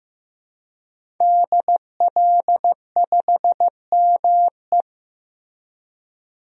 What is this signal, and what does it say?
Morse code, heard as a keyed tone, about 15 wpm, 700 Hz. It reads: DL5ME
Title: DXCC-100 Call Signs - 5 Characters - 15wpm